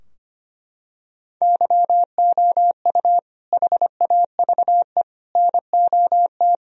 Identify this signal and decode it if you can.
Morse code, heard as a keyed tone, about 25 wpm, 700 Hz. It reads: YOU HAVE NOT
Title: Sentences from Top 1000 Words - Review Part 2 - 25wpm